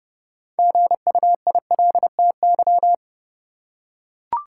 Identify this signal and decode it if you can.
Morse code, heard as a keyed tone, about 30 wpm, 700 Hz. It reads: GUILTY
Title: Six Letter Words - 30wpm